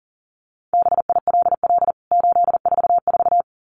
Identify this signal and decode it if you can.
Morse code, heard as a keyed tone, about 40 wpm, 700 Hz. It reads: BILL 844
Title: CWT Exchange 40wpm